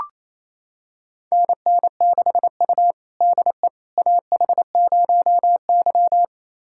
Morse code, heard as a keyed tone, about 28 wpm, 700 Hz. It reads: NN6U DE AH0Y
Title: USA Callsign DE Callsign - 4 Characters - 28wpm